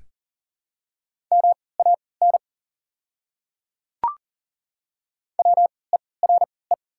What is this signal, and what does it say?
Morse code, heard as a keyed tone, about 40 wpm, spaced as for 23 wpm, 700 Hz. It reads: MAN WERE
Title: 3x Character Spacing - Top 500 Words - 40wpm